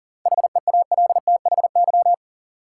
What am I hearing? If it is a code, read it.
Morse code, heard as a keyed tone, about 40 wpm, 700 Hz. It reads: HEALTHY
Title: Sentences from Top 1000 Words - Review Part 5 - New Word Drill - 40wpm